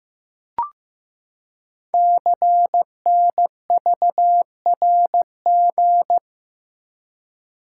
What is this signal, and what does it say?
Morse code, heard as a keyed tone, about 15 wpm, 700 Hz. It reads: CNVRG
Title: Code Groups - 5 Letters - 15wpm